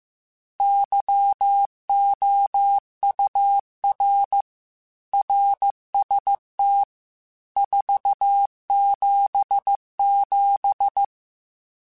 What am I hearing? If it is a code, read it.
Morse code, heard as a keyed tone, about 15 wpm, 800 Hz. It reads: YOUR RST 477
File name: RSTs / RST-mp3s / Your RSTs @13